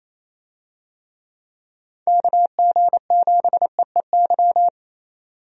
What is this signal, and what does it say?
Morse code, heard as a keyed tone, about 28 wpm, 700 Hz. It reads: KG7EEY
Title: USA Call Signs - 6 Characters - 28wpm